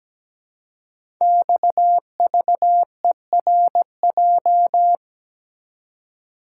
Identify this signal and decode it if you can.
Morse code, heard as a keyed tone, about 17 wpm, 700 Hz. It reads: XVERJ
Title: Code Groups - 5 Letters - 17wpm